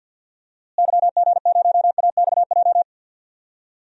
Morse code, heard as a keyed tone, about 52 wpm, 700 Hz. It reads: YG0AXJ
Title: DXCC-100 Call Signs - 6 Characters - Repeated 3x - 50wpm